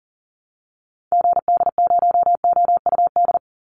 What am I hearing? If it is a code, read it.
Morse code, heard as a keyed tone, about 40 wpm, 700 Hz. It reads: GD0OUD
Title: Call Signs Repeated 2x - 40wpm